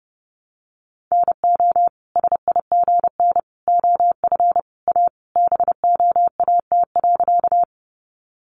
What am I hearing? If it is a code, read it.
Morse code, heard as a keyed tone, about 30 wpm, 700 Hz. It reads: NO SIGN OF A BOAT.
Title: Sentences from Top 1000 Words 30wpm